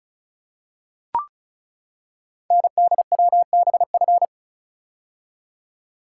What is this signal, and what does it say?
Morse code, heard as a keyed tone, about 36 wpm, 690 Hz. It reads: NDWBF
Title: Code Groups - 5 Letters - 35wpm